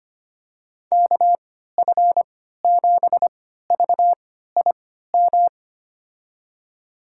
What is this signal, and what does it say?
Morse code, heard as a keyed tone, about 25 wpm, spaced as for 14 wpm, 700 Hz. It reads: KF7VIM
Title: USA Call Signs - 6 Characters - 3x Character Spacing - 25wpm